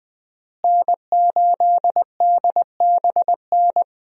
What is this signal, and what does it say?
Morse code, heard as a keyed tone, about 20 wpm, 700 Hz. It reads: N8DBN
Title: USA Call Signs - 5 Characters - Repeated 2x - 20wpm